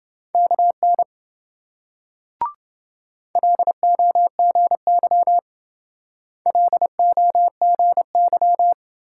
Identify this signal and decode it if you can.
Morse code, heard as a keyed tone, about 26 wpm, 700 Hz. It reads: KN LOGY LOGY